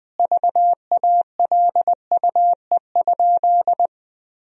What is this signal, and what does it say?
Morse code, heard as a keyed tone, about 21 wpm, 700 Hz. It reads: VALUE?
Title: Sentences from Top 1000 Words - Review Part 3 - 20wpm